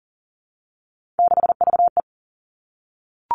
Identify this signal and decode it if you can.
Morse code, heard as a keyed tone, about 40 wpm, 700 Hz. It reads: 6VE
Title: Three Challenging Letter-Number Combinations 40wpm